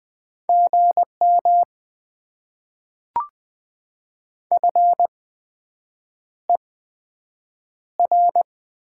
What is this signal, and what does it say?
Morse code, heard as a keyed tone, about 20 wpm, spaced as for 7 wpm, 700 Hz. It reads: GM FER
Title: Top 20 QSO Elements - 8x Character Spacing - 20wpm